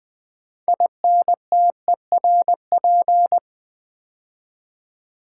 Morse code, heard as a keyed tone, about 20 wpm, 700 Hz. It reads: INTERP
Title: Build A Word - 3 Letters At A Time - 9-Letter Words - 20wpm